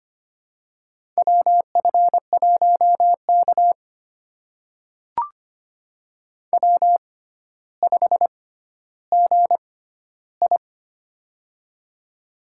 Morse code, heard as a keyed tone, about 25 wpm, spaced as for 13 wpm, 700 Hz. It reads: WF1K W5GI
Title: USA Call Signs - 4 Characters - 6x Character Spacing - 25wpm